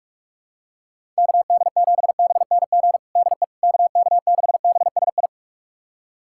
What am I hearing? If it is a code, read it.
Morse code, heard as a keyed tone, about 46 wpm, 700 Hz. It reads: KD7BNG DE KK6BSI
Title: USA Callsign DE Callsign - 6 Characters - 45wpm